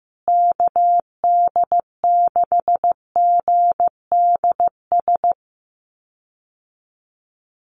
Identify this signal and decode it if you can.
Morse code, heard as a keyed tone, about 15 wpm, 700 Hz. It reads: KD6GDS
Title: USA Call Signs Repeated 3x 15wpm